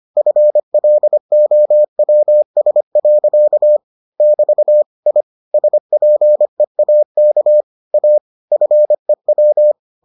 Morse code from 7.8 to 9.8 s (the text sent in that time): A FEW